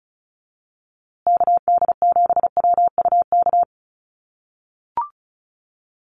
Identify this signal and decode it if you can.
Morse code, heard as a keyed tone, about 34 wpm, 700 Hz. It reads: KD7WUK
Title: USA Call Signs Repeated 3x 35wpm